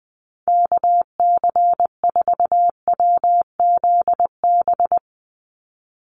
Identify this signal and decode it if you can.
Morse code, heard as a keyed tone, about 20 wpm, 700 Hz. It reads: KC4WZB
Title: NA SOTA Activator Callsigns 20wpm